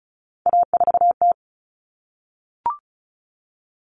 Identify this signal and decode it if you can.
Morse code, heard as a keyed tone, about 34 wpm, 700 Hz. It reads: A4T
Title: All Cut Numbers - 3-Digit 35wpm